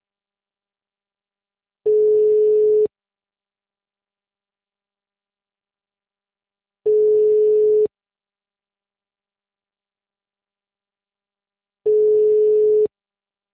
Ringback tone (Europe, 425 Hz).
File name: out- -2002- - - .217